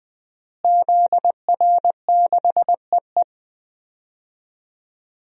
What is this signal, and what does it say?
Morse code, heard as a keyed tone, about 20 wpm, 700 Hz. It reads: ZR6EE
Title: DXCC-100 Call Signs - 5 Characters - Repeated 2x - 20wpm